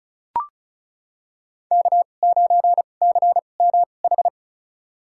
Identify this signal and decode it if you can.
Morse code, heard as a keyed tone, about 36 wpm, spaced as for 25 wpm, 700 Hz. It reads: K9CMH
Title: USA Call Signs - 2x Character Spacing - 35wpm